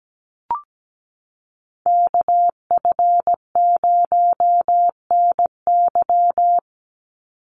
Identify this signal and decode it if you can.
Morse code, heard as a keyed tone, about 17 wpm, 700 Hz. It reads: KF0NY
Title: USA Call Signs 17wpm